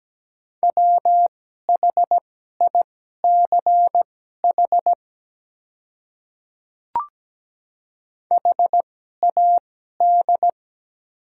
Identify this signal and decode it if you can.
Morse code, heard as a keyed tone, about 17 wpm, spaced as for 12 wpm, 700 Hz. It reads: WHICH HAD
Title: 2x Character Spacing - Top 300 Words - 17wpm